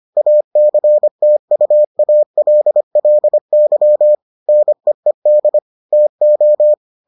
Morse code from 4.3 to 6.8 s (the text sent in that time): NEED TO